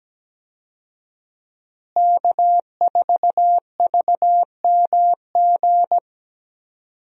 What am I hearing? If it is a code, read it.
Morse code, heard as a keyed tone, about 17 wpm, 700 Hz. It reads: K4VMG
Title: USA Call Signs - 5 Characters - Repeated 2x - 17wpm